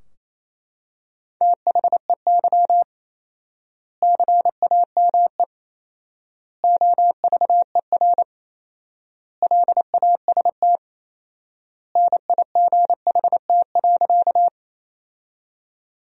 Morse code, heard as a keyed tone, about 28 wpm, 700 Hz. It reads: THEY CAME OVER LAST NIGHT.
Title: 4x Word-Spacing - Sentences from Top 300 Words - 28wpm